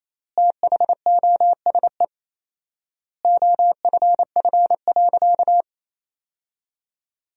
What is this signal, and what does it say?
Morse code, heard as a keyed tone, about 28 wpm, 700 Hz. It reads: THOSE OFF.
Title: 4x Word-Spacing - Sentences from Top 500 Words - 28wpm